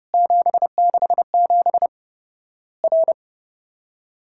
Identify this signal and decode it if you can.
Morse code, heard as a keyed tone, about 30 wpm, 700 Hz. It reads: 767 R
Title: Worked All Europe DX Contest - QTC Exchange - 30wpm